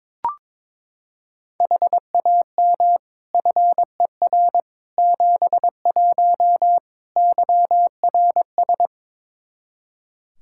Morse code, heard as a keyed tone, about 22 wpm, 700 Hz. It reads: HAM FER 71 YRS
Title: K1USN SST QSO Practice 22wpm